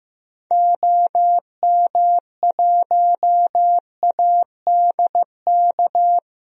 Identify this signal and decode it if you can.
Morse code, heard as a keyed tone, about 15 wpm, 690 Hz. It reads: OM1ADK
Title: DXCC-100 Call Signs - 6 Characters - 15wpm